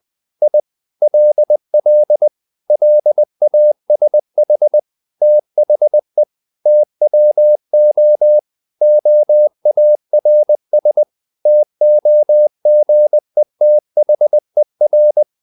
Morse code, heard as a keyed tone, about 20 wpm, 600 Hz. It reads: I LL LASH THE TWO OARS TOGETHER
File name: the old man and the sea part 2 20wpm